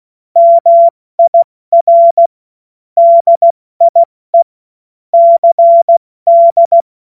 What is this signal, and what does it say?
Morse code, heard as a keyed tone, about 16 wpm, 680 Hz. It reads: MIR DIE CD
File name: Morsekurs / Witze / Witz 10. Januar 2023 70 F80